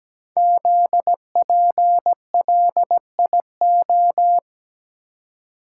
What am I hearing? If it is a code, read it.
Morse code, heard as a keyed tone, about 17 wpm, 700 Hz. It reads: ZPLIO
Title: Code Groups - 5 Letters - 17wpm